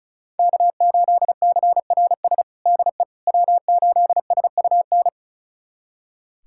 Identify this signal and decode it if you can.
Morse code, heard as a keyed tone, about 36 wpm, 700 Hz. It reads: K8CRS DE W8SUN
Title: USA Callsign DE Callsign - 5 Characters - 35wpm